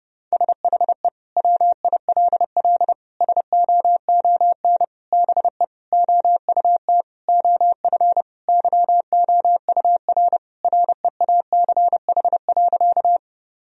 Morse code, heard as a keyed tone, about 32 wpm, 700 Hz. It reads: SHE WILL SOON BE OUT OF YOUR REACH.
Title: Sentences from Top 1000 Words - Review Part 2 - 30wpm